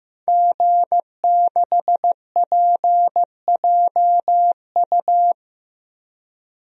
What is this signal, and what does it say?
Morse code, heard as a keyed tone, about 15 wpm, 700 Hz. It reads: G6PJU
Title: DXCC-100 Call Signs - 5 Characters - 15wpm